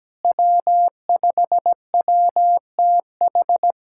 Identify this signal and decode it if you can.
Morse code, heard as a keyed tone, about 17 wpm, 700 Hz. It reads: W5WTH
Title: K1USN SST QSO Practice - 17wpm